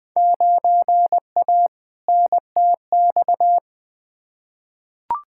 Morse code, heard as a keyed tone, about 20 wpm, 700 Hz. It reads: 9A NTX
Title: Field Day Exchanges Encore - 20wpm